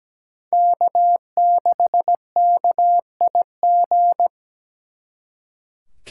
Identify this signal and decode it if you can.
Morse code, heard as a keyed tone, about 20 wpm, 700 Hz. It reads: K6KIG
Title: Speed Racing - USA 1x3 Call Signs 17wpm